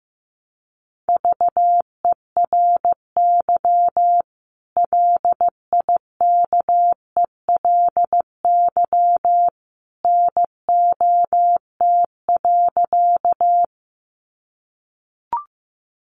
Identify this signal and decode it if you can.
Morse code, heard as a keyed tone, about 15 wpm, 700 Hz. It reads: VERY LIKELY NOT.
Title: Sentences from Top 600 Words 15wpm